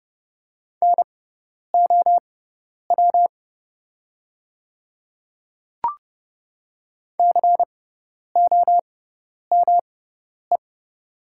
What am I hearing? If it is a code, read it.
Morse code, heard as a keyed tone, about 30 wpm, spaced as for 10 wpm, 690 Hz. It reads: NOW COME
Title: 6x Character Spacing - Top 100 Words - 30wpm